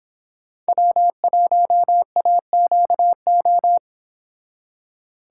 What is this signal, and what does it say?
Morse code, heard as a keyed tone, about 25 wpm, 700 Hz. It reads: W1AQO